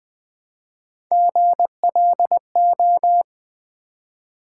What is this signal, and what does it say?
Morse code, heard as a keyed tone, about 20 wpm, 700 Hz. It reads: GLO
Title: Build A Word - 1 Letter At A Time - 5-Letter Words - 20wpm